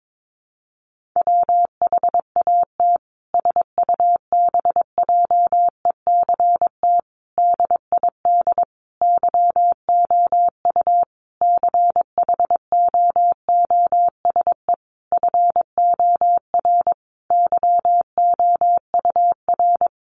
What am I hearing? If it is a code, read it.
Morse code, heard as a keyed tone, about 22 wpm, 700 Hz. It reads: WHAT SUBJECT DID YOU CHOOSE FOR YOUR
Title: Sentences from Top 1500 Words 22wpm